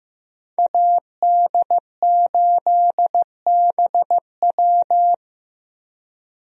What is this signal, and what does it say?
Morse code, heard as a keyed tone, about 15 wpm, 700 Hz. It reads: AD8BW
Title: USA Call Signs - 5 Characters - Repeated 2x - 15wpm